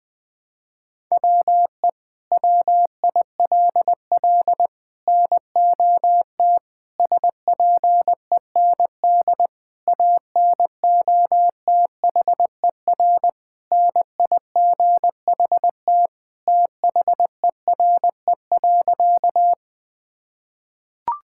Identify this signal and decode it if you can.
Morse code, heard as a keyed tone, about 21 wpm, 700 Hz. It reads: WE WILL NOT SPEND ANOTHER NIGHT THERE.
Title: Sentences from Top 1000 Words - Review Part 5 - 20wpm